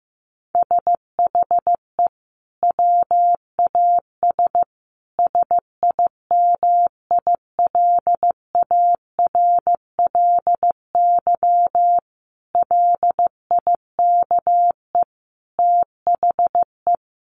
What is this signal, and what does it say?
Morse code, heard as a keyed tone, about 15 wpm, 700 Hz. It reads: SHE WAS SIMILARLY LIKE THE
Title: Sentences from Top 1400 Words 15wpm